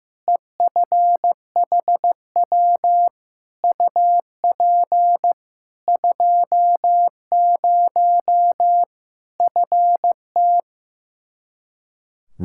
Morse code, heard as a keyed tone, about 16 wpm, 700 Hz. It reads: EFHW UP 20 FT
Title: Practice QSO - Exchange 2 - Info - 15wpm